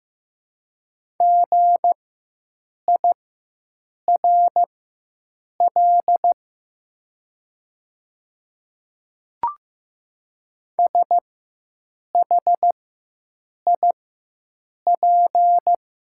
Morse code, heard as a keyed tone, about 15 wpm, spaced as for 7 wpm, 700 Hz. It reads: GIRL SHIP
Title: 4x Character Spacing - Top 500 Words - 15wpm